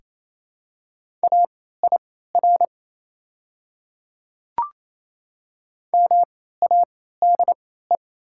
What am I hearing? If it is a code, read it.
Morse code, heard as a keyed tone, about 28 wpm, spaced as for 16 wpm, 700 Hz. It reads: AIR MADE
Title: 3x Character Spacing - Top 300 Words - 28wpm